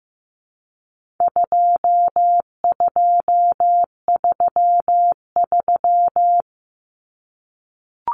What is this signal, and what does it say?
Morse code, heard as a keyed tone, about 15 wpm, 700 Hz. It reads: 2233
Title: Common Cut Numbers - 4-Digit 15wpm